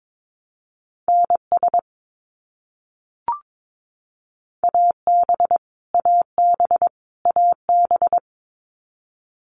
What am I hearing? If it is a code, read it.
Morse code, heard as a keyed tone, about 22 wpm, 700 Hz. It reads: NS AB AB AB
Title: Canadian Provinces and Territories - Repeated 3x - 22wpm